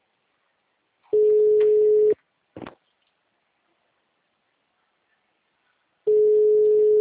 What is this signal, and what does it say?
Ringback tone (Europe, 425 Hz).
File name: out- -2003- - - .606